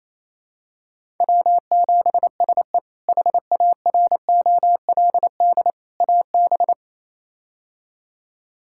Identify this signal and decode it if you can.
Morse code, heard as a keyed tone, about 28 wpm, 700 Hz. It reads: W7SE HAROLD AB
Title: K1USN SST QSO Practice - 28wpm